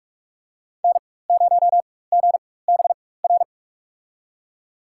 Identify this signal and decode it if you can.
Morse code, heard as a keyed tone, about 43 wpm, spaced as for 21 wpm, 700 Hz. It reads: N0GBR
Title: USA Call Signs - 5 Characters - 4x Character Spacing - 45wpm